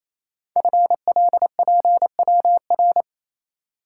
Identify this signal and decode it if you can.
Morse code, heard as a keyed tone, about 27 wpm, 700 Hz. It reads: FLPWR
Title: Code Groups - 5 Letters - 28wpm